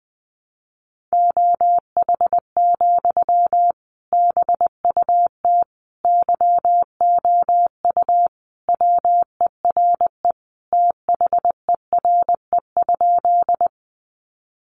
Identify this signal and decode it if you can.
Morse code, heard as a keyed tone, about 20 wpm, 700 Hz. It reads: OH, BUT YOU WERE THERE?
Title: Sentences from Top 500 Words 20wpm